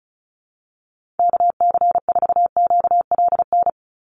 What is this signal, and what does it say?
Morse code, heard as a keyed tone, about 34 wpm, 700 Hz. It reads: KC4QLN
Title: USA Call Signs 35wpm